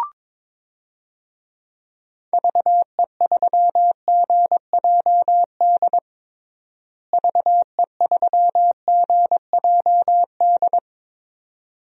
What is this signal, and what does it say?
Morse code, heard as a keyed tone, about 22 wpm, 700 Hz. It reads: VE3GJD VE3GJD
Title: DXCC-100 Call Signs - 6 Characters - Repeated 2x - 22wpm